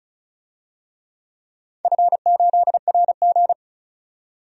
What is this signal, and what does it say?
Morse code, heard as a keyed tone, about 36 wpm, 700 Hz. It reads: F8RG
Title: DXCC-100 Call Signs - 4 Characters - 35wpm